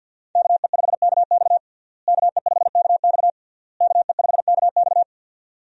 Morse code, heard as a keyed tone, about 52 wpm, 700 Hz. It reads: KE5KX KE5KX KE5KX
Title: USA Call Signs - 5 Characters - Repeated 3x - 50wpm